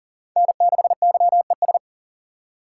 Morse code, heard as a keyed tone, about 43 wpm, 690 Hz. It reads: N6YES
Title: USA Call Signs - 5 Characters - 6x Character Spacing - 40wpm